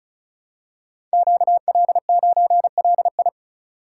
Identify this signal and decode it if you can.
Morse code, heard as a keyed tone, about 36 wpm, 700 Hz. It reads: QL9LI
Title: Code Groups - 5 Letter:Numbers - 35wpm